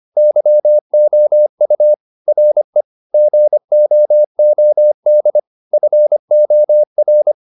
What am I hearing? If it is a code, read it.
Morse code, heard as a keyed tone, about 25 wpm, 600 Hz. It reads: YOU RE GOOD FOR